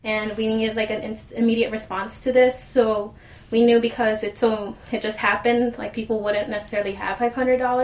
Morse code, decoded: none

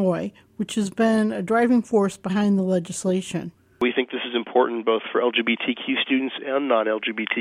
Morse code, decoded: none

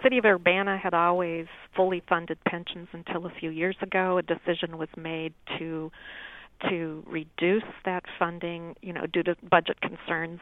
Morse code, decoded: none

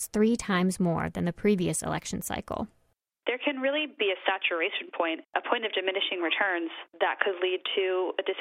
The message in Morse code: none

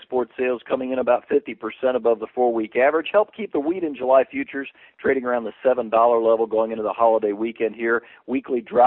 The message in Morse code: none

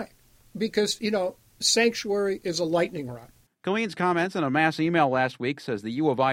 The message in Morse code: none

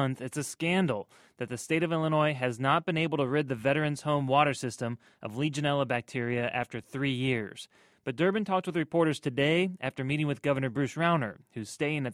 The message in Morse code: none

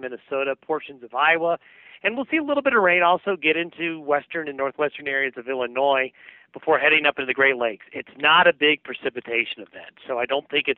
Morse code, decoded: none

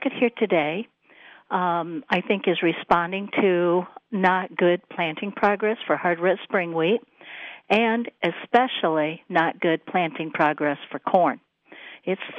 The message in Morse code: none

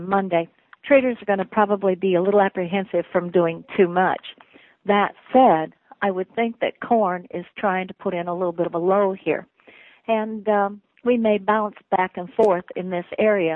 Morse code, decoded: none

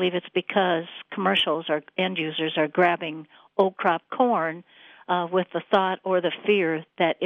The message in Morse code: none

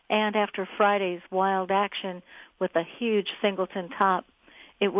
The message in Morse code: none

